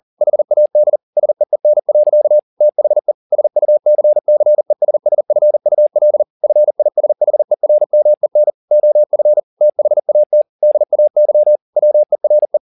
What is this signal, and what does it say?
Morse code, heard as a keyed tone, about 40 wpm, 600 Hz. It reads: HAD SEEN. THE SUCCESSFUL FISHERMEN OF THAT DAY WERE